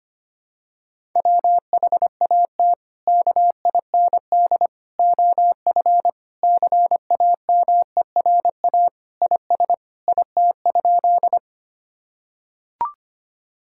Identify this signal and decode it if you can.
Morse code, heard as a keyed tone, about 25 wpm, 700 Hz. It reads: WHAT KIND OF CAMERA IS IT?
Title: Sentences from Top 1000 Words - Review Part 5 - 25wpm